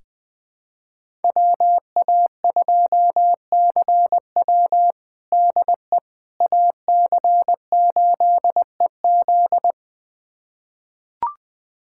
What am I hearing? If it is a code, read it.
Morse code, heard as a keyed tone, about 20 wpm, 700 Hz. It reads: WA2CW DE AC8EZ
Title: USA Callsign DE Callsign - 5 Characters - 20wpm